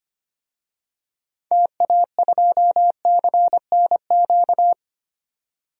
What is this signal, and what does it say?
Morse code, heard as a keyed tone, about 25 wpm, 700 Hz. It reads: TA2CNQ
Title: DXCC-100 Call Signs - 6 Characters - 25wpm